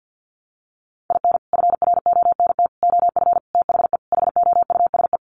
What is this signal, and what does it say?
Morse code, heard as a keyed tone, about 50 wpm, 700 Hz. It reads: IN FRONT OF THE HOUSE
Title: Sets of 5 Words Encore2 50wpm